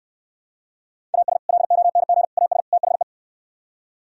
Morse code, heard as a keyed tone, about 67 wpm, 700 Hz. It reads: AS LONG AS THE